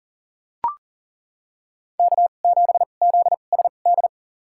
Morse code, heard as a keyed tone, about 40 wpm, spaced as for 29 wpm, 700 Hz. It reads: K7ZSD
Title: USA Call Signs - 5 Characters - 2x Character Spacing - 40wpm